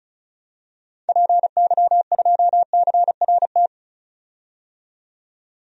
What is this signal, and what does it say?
Morse code, heard as a keyed tone, about 36 wpm, 700 Hz. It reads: PY2CRT
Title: DXCC-100 Call Signs - 6 Characters - 35wpm